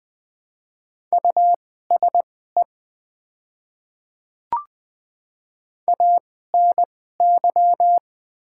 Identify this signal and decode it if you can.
Morse code, heard as a keyed tone, about 20 wpm, spaced as for 14 wpm, 700 Hz. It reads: USE ANY
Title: 2x Character Spacing - Top 200 Words - 20wpm